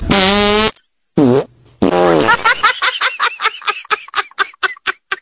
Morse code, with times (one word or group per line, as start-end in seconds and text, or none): none